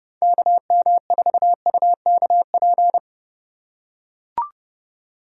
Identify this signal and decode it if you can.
Morse code, heard as a keyed tone, about 27 wpm, 700 Hz. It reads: KM4UKP